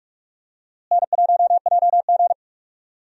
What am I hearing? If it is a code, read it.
Morse code, heard as a keyed tone, about 41 wpm, 700 Hz. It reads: N1JG
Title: USA Call Signs - 4 Characters - Repeated 3x - 45wpm